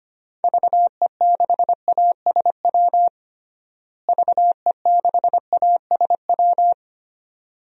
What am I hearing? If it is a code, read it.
Morse code, heard as a keyed tone, about 25 wpm, 700 Hz. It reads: VE6ASW VE6ASW
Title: DXCC-100 Call Signs - 6 Characters - Repeated 2x - 25wpm